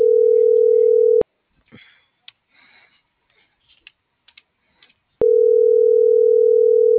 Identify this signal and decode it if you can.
Ringback tone (North America), 440 + 480 Hz.
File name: CENTRAL TERRESTRE / out els..laz.elo.mo.24